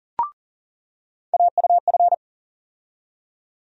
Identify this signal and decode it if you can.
Morse code, heard as a keyed tone, about 40 wpm, 700 Hz. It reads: AUF
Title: USA Call Sign Postfixes - 3 Letters 40wpm